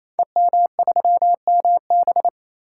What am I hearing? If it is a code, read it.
Morse code, heard as a keyed tone, about 28 wpm, 700 Hz. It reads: EM3MB
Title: Code Groups - 5 Letter:Numbers - 28wpm